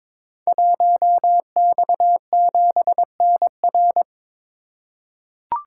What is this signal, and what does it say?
Morse code, heard as a keyed tone, about 22 wpm, 700 Hz. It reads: 1X7NR
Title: Code Groups - 5 Letter:Numbers - 22wpm